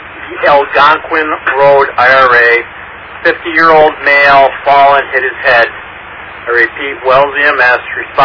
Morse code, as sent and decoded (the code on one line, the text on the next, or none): none